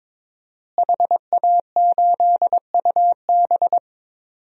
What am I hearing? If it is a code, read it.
Morse code, heard as a keyed tone, about 22 wpm, 700 Hz. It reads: HA8UB
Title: DXCC-100 Call Signs - 5 Characters - Repeated 3x - 22wpm